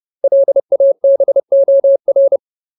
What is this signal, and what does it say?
Morse code, heard as a keyed tone, about 30 wpm, 550 Hz. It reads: LABOR
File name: Ultimate Word List / 30WPM Ultimate